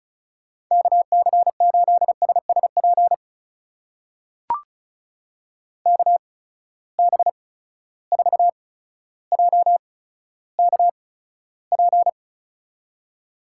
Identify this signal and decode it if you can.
Morse code, heard as a keyed tone, about 36 wpm, spaced as for 15 wpm, 700 Hz. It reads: KC8SSP KB4JKP
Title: USA Call Signs - 6 Characters - 8x Character Spacing - 35wpm